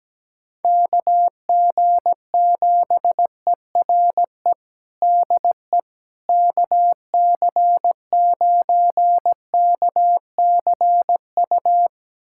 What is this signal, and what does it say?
Morse code, heard as a keyed tone, about 17 wpm, 700 Hz. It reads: KG7ERE DE KC9KCU
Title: USA Callsign DE Callsign 17wpm